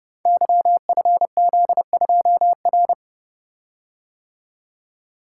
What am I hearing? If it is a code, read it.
Morse code, heard as a keyed tone, about 30 wpm, 700 Hz. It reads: YFZ2R
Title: Code Groups - 5 Letter:Numbers - 30wpm